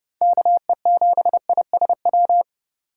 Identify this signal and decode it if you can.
Morse code, heard as a keyed tone, about 30 wpm, 700 Hz. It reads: KE7ISW
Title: USA Call Signs - 3x Character Spacing - 30wpm